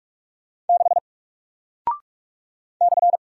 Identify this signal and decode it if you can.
Morse code, heard as a keyed tone, about 46 wpm, 700 Hz. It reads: 6 /
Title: Single Character (Letter:Number:Punctuation:Prosign) Alt 45wpm